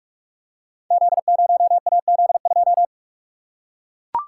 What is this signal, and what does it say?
Morse code, heard as a keyed tone, about 44 wpm, 700 Hz. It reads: Z0AZ2
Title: Code Groups - 5 Letter:Numbers - 45wpm